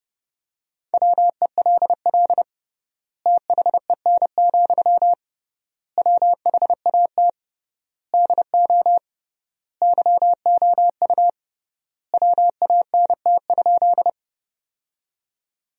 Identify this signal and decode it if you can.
Morse code, heard as a keyed tone, about 30 wpm, 700 Hz. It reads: WELL THEN, WHAT DO YOU WANT?
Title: 3x Word-Spacing - Sentences from Top 200 Words - 30wpm